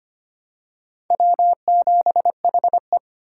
Morse code, heard as a keyed tone, about 25 wpm, 700 Hz. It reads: W7HE
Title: USA Call Signs - 4 Characters - 4x Character Spacing - 25wpm